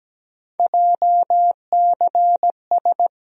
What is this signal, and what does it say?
Morse code, heard as a keyed tone, about 17 wpm, 700 Hz. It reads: JCS
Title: USA Call Sign Postfixes - 3 Letters 17wpm